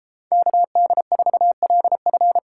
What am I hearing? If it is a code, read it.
Morse code, heard as a keyed tone, about 30 wpm, 700 Hz. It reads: KD4LF